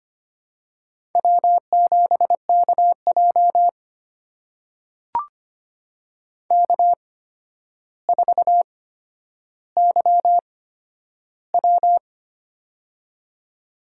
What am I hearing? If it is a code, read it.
Morse code, heard as a keyed tone, about 25 wpm, spaced as for 11 wpm, 700 Hz. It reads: W7KJ K4YW
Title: USA Call Signs - 4 Characters - 8x Character Spacing - 25wpm